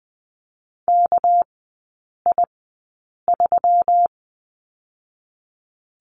Morse code, heard as a keyed tone, about 20 wpm, spaced as for 8 wpm, 690 Hz. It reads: KI3
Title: 205 Lesson 33e - Letter j - Three Characters 20wpm